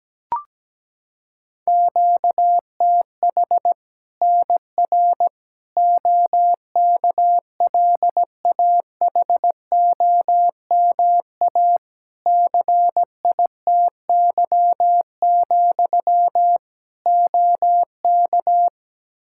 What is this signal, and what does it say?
Morse code, heard as a keyed tone, about 17 wpm, 700 Hz. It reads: QTH NR OKLAHOMA CITY, OK
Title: Practice QSO - Exchange 1 - Beginning - 17wpm